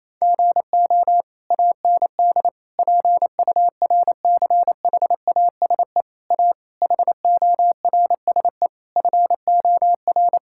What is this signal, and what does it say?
Morse code, heard as a keyed tone, about 28 wpm, 700 Hz. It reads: GO AND PURCHASE A HORSE FOR
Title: Sentences from Top 1000 Words - Review Part 5 - 28wpm